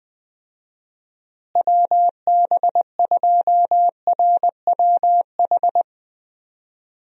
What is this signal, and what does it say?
Morse code, heard as a keyed tone, about 20 wpm, 700 Hz. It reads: WB2RWH
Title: USA Call Signs - 6 Characters - Repeated 3x - 20wpm